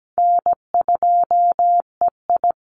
Morse code, heard as a keyed tone, about 17 wpm, 700 Hz. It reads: N2EI
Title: Easy Call Signs Repeated 2x - 17wpm